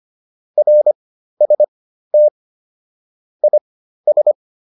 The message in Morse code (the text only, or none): RST IS